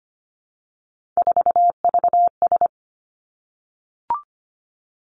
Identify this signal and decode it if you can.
Morse code, heard as a keyed tone, about 25 wpm, 700 Hz. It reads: 4VS
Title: Three Challenging Letter-Number Combinations 25wpm